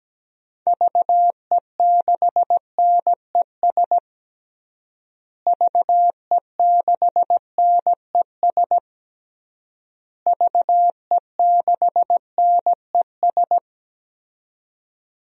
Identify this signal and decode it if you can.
Morse code, heard as a keyed tone, about 17 wpm, 700 Hz. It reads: VE6NES VE6NES VE6NES
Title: DXCC-100 Call Signs - 6 Characters - Repeated 3x - 17wpm